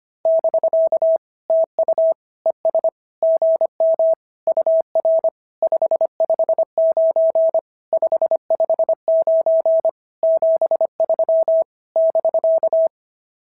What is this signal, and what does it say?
Morse code, heard as a keyed tone, about 25 wpm, 690 Hz. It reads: <BK> TU ES GM UR 559 559 73 <BK>
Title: SOTA QSO Practice 25wpm